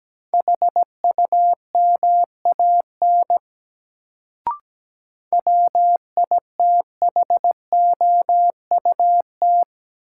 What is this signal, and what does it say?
Morse code, heard as a keyed tone, about 17 wpm, 700 Hz. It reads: HUMAN WITHOUT
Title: Sentences from Top 600 Words - New Word Drill - 17wpm